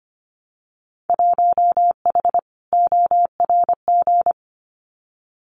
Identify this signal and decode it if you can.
Morse code, heard as a keyed tone, about 25 wpm, 700 Hz. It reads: 1H ORG
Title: Winter Field Day Exchanges 25wpm